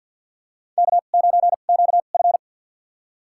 Morse code, heard as a keyed tone, about 51 wpm, spaced as for 36 wpm, 700 Hz. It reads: K9QF
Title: USA Call Signs - 2x Character Spacing - 50wpm